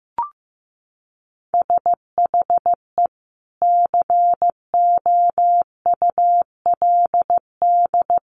Morse code, heard as a keyed tone, about 15 wpm, 700 Hz. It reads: SHE COULD
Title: Sentences from Top 1100 Words 15wpm